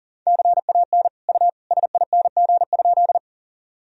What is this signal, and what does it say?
Morse code, heard as a keyed tone, about 37 wpm, 700 Hz. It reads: CAN U SING?
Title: Sentences from Top 400 Words and Abbreviations 40wpm